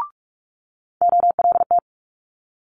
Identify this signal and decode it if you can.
Morse code, heard as a keyed tone, about 46 wpm, 700 Hz. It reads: OLT
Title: Multi-letter Phonemes Quick Fire 45wpm